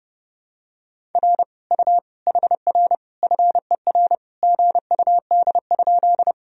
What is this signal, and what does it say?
Morse code, heard as a keyed tone, about 30 wpm, 700 Hz. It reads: R U HR FER GUD?
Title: Sentences from Top 200 Words and Abbreviations 30wpm